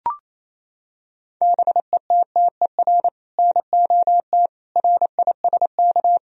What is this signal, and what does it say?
Morse code, heard as a keyed tone, about 28 wpm, 700 Hz. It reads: BETTER NOT RISK
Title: Sentences from Top 1000 Words - Review Part 4 - 28wpm